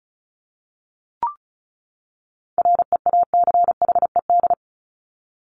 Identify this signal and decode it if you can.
Morse code, heard as a keyed tone, about 34 wpm, 700 Hz. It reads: REACHED
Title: ED Suffix - Top 100 Words with Base Words 35wpm